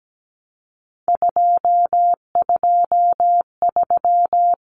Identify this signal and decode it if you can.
Morse code, heard as a keyed tone, about 17 wpm, 700 Hz. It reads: 223
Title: Common Cut Numbers - 3-Digit 17wpm